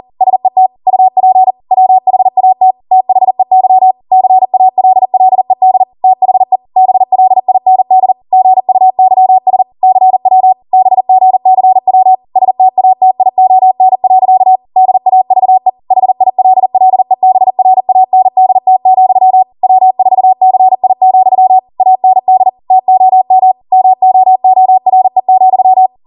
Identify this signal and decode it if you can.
Morse code, heard as a keyed tone, about 40 wpm, 750 Hz. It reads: SET UP WHAT THEY CALLED THE BLIND GUYS CW BGCW STATION. DAVE HILLEBRANDT, W4CI, AND TOM MOORE,